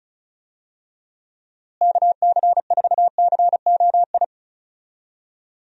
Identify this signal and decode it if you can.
Morse code, heard as a keyed tone, about 36 wpm, 700 Hz. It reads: KC4COI
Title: USA Call Signs - 6 Characters - Repeated 2x - 35wpm